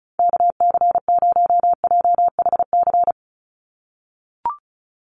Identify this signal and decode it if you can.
Morse code, heard as a keyed tone, about 34 wpm, 700 Hz. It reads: KC0JHC
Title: USA Call Signs 35wpm